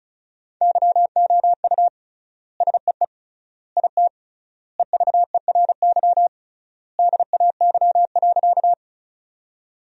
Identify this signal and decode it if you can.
Morse code, heard as a keyed tone, about 34 wpm, 700 Hz. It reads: YOU SEE IT EVERY DAY.
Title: 3x Word-Spacing - Sentences from Top 200 Words - 35wpm